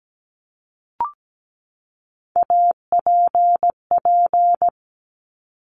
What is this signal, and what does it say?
Morse code, heard as a keyed tone, about 17 wpm, 690 Hz. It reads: APP
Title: Top Three Letter Combinations in Words 17wpm